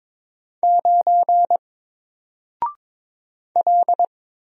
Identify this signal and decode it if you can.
Morse code, heard as a keyed tone, about 22 wpm, 700 Hz. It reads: 9 L
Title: Single Character (Letter:Number:Punctuation:Prosign) Alt 22wpm